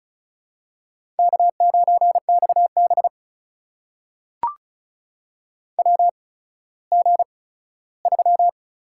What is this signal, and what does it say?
Morse code, heard as a keyed tone, about 36 wpm, spaced as for 17 wpm, 700 Hz. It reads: K9XB WG3
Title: USA Call Signs - 4 Characters - 8x Character Spacing - 35wpm